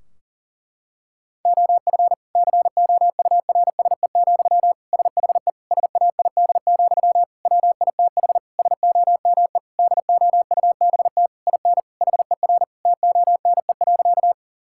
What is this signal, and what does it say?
Morse code, heard as a keyed tone, about 40 wpm, 700 Hz. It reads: OF COURSE, SHE SAID, WITH SOME DOUBT IN HER TONE.
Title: Sentences from Top 1600 Words 40wpm